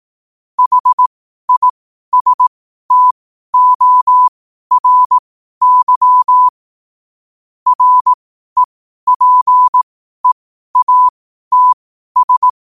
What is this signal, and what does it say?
Morse code, heard as a keyed tone, about 18 wpm, spaced as for 12 wpm, 1000 Hz. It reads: HISTORY REPEATS
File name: qotdfiles / 12wpm